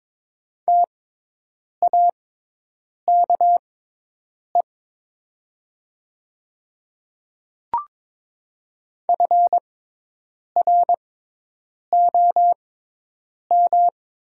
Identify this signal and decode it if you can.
Morse code, heard as a keyed tone, about 22 wpm, spaced as for 8 wpm, 700 Hz. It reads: TAKE FROM
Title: 6x Character Spacing - Top 300 Words - 22wpm